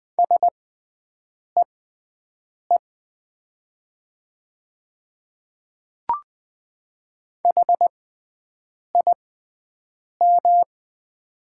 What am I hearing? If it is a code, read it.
Morse code, heard as a keyed tone, about 20 wpm, spaced as for 7 wpm, 700 Hz. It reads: SEE HIM
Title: 6x Character Spacing - Top 100 Words - 20wpm